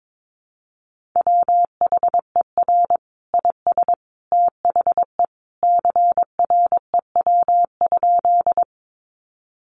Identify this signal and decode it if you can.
Morse code, heard as a keyed tone, about 22 wpm, 700 Hz. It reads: WHER IS THE CREW?
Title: Sentences from Top 1500 Words 22wpm